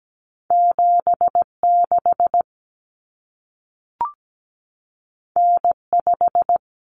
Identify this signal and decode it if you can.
Morse code, heard as a keyed tone, about 17 wpm, 700 Hz. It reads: 76 N5
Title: Common Cut Numbers - 2-Digit 17wpm